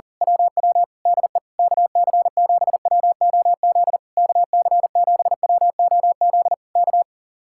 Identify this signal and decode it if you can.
Morse code, heard as a keyed tone, about 37 wpm, 700 Hz. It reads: WW DE KC7WOZ KC7WOZ K
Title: Speed-Racing Common QSO Phrases Encore - 40wpm